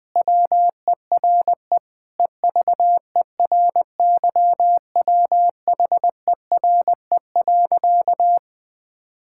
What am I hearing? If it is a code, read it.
Morse code, heard as a keyed tone, about 20 wpm, 700 Hz. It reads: WERE EVERYWHERE.
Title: Sentences with 8-Word Phrases - 20wpm